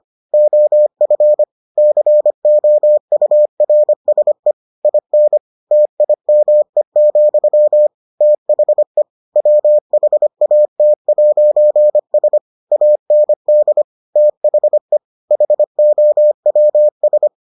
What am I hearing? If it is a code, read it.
Morse code, heard as a keyed tone, about 25 wpm, 600 Hz. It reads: OF COURSE IN TIME, THE WHAT'S AND THE HOWS